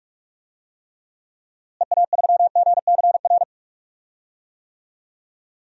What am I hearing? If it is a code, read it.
Morse code, heard as a keyed tone, about 44 wpm, 700 Hz. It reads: EA3GCR
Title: DXCC-100 Call Signs - 6 Characters - Repeated 3x - 45wpm